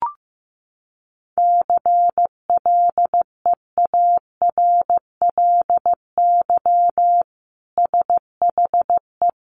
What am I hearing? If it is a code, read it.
Morse code, heard as a keyed tone, about 15 wpm, 700 Hz. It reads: CLEARLY SHE
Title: Sentences from Top 1300 Words 15wpm